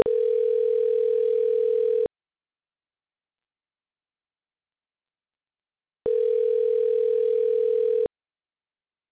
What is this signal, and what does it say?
Ringback tone (North America), 440 + 480 Hz.